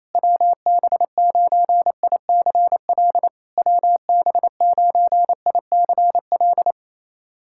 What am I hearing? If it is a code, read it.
Morse code, heard as a keyed tone, about 29 wpm, 700 Hz. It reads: WB9ICL WB9ICL
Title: SOTA QSO Practice 28wpm